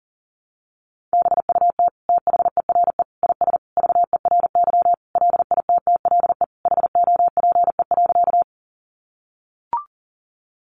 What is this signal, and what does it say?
Morse code, heard as a keyed tone, about 40 wpm, 700 Hz. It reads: BUT THERE IS VERY LITTLE HOPE.
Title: Sentences from Top 1000 Words 40wpm